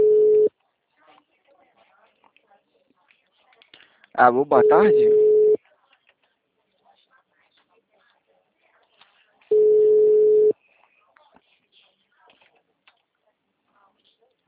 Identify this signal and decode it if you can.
Ringback tone (Europe, 425 Hz), with speech.